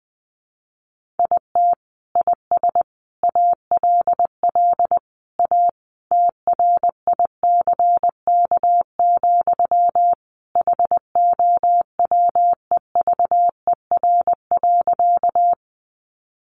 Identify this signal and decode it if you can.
Morse code, heard as a keyed tone, about 20 wpm, 700 Hz. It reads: IT IS ALL A TRICK, HOWEVER.
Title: Sentences from Top 1500 Words 20wpm